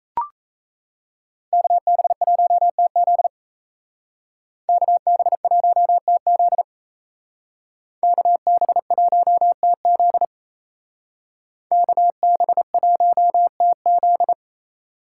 Morse code, heard as a keyed tone, about 36 wpm, 700 Hz. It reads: KB1TZ KB1TZ KB1TZ KB1TZ